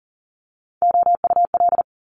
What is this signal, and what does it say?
Morse code, heard as a keyed tone, about 40 wpm, 700 Hz. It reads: OUL
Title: Multi-letter Phonemes 40wpm